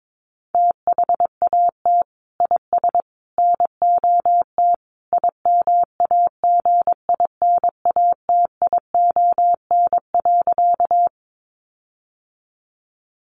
Morse code, heard as a keyed tone, about 22 wpm, 700 Hz. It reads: THAT IS NOT IMAGINATION.
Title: Sentences from Top 1300 Words 22wpm